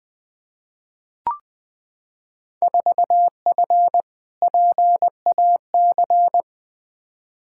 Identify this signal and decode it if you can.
Morse code, heard as a keyed tone, about 20 wpm, 700 Hz. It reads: 4F PAC
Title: Field Day Exchanges Encore - 20wpm